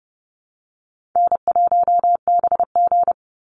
Morse code, heard as a keyed tone, about 30 wpm, 700 Hz. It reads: N1BG
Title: Easy Call Signs - 30wpm